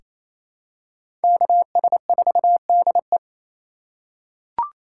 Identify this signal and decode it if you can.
Morse code, heard as a keyed tone, about 28 wpm, 700 Hz. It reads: KS4DE
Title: USA Call Signs - 4x Character Spacing - 28wpm